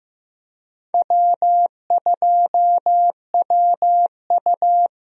Morse code, heard as a keyed tone, about 15 wpm, 700 Hz. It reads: W2WU
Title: USA Call Signs - 4 Characters - 15wpm